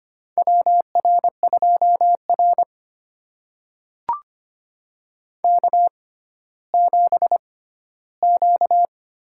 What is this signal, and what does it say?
Morse code, heard as a keyed tone, about 25 wpm, spaced as for 14 wpm, 700 Hz. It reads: WR2R K7Q
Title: USA Call Signs - 4 Characters - 6x Character Spacing - 25wpm